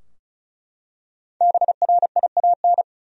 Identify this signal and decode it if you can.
Morse code, heard as a keyed tone, about 36 wpm, 700 Hz. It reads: BRIAN
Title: USA Common Names - Max 5 Letters - 35wpm